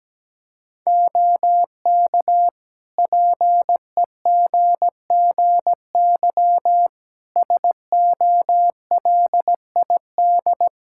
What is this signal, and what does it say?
Morse code, heard as a keyed tone, about 17 wpm, 700 Hz. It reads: OK PEGGY SOLID
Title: Practice QSO - Exchange 3 - More Info - 17wpm